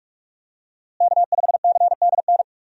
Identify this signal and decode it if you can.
Morse code, heard as a keyed tone, about 41 wpm, 700 Hz. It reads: K5CDN
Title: USA Call Signs - 5 Characters - Repeated 3x - 45wpm